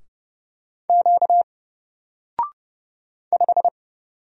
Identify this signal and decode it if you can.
Morse code, heard as a keyed tone, about 30 wpm, 700 Hz. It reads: Q 5
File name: Single-Letter-Number-30wpm